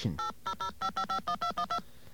DTMF, heard as telephone touch tones, tones of about 90 ms. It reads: 8005551212